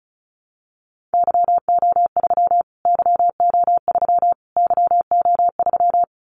Morse code, heard as a keyed tone, about 34 wpm, 700 Hz. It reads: YO3 YO3 YO3
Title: Callsign Prefixes for Top 25 DXCC - Repeated 3x - 35wpm